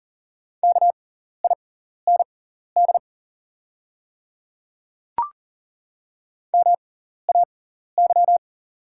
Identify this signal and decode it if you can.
Morse code, heard as a keyed tone, about 38 wpm, spaced as for 14 wpm, 690 Hz. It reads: KIND MAY
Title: 6x Character Spacing - Top 200 Words - 40wpm